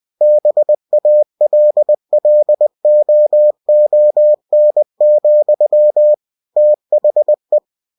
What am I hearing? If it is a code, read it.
Morse code, heard as a keyed tone, about 20 wpm, 600 Hz. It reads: BALLOON, THE
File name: cw-books / winnie-the-pooh / winnie-20wpm-page-0005